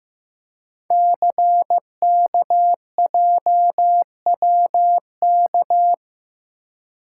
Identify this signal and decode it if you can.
Morse code, heard as a keyed tone, about 15 wpm, 700 Hz. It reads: CKJWK
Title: Code Groups - 5 Letters - 15wpm